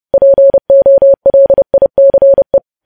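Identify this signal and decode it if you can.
Morse code, heard as a keyed tone, about 30 wpm, 550 Hz. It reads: POLICE
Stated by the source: Morse news headlines sent at 30 wpm, opening with vvv.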